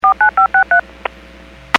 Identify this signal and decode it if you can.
DTMF, heard as telephone touch tones, tones of about 110 ms. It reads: 49563
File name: WPS / 00801 02-32 WPS 451.25 WPS